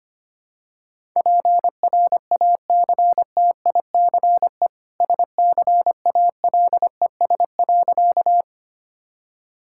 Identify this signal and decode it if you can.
Morse code, heard as a keyed tone, about 25 wpm, 700 Hz. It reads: PRACTICE SCALES.
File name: Easy Sentences with Hints - 25wpm